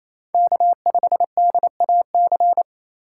Morse code, heard as a keyed tone, about 28 wpm, 700 Hz. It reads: K5DAC
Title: USA Call Signs - 5 Characters - 4x Character Spacing - 28wpm